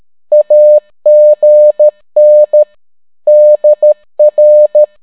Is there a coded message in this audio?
Yes: Morse code from 0.3 to 4.8 s, heard as a keyed tone, about 13 wpm, 600 Hz. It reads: AGN DR